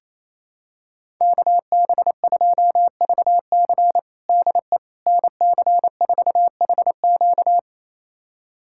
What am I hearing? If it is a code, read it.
Morse code, heard as a keyed tone, about 28 wpm, 700 Hz. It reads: KB2VC DE NC4HQ
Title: USA Callsign DE Callsign - 5 Characters - 28wpm